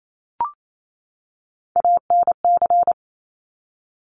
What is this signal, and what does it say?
Morse code, heard as a keyed tone, about 28 wpm, 700 Hz. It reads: ANC
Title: Top Three Letter Combinations in Words 28wpm